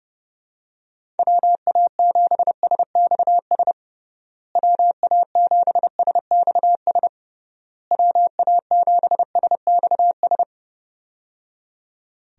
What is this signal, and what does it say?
Morse code, heard as a keyed tone, about 30 wpm, 700 Hz. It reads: WA7SXS WA7SXS WA7SXS
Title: USA Call Signs - 6 Characters - Repeated 3x - 30wpm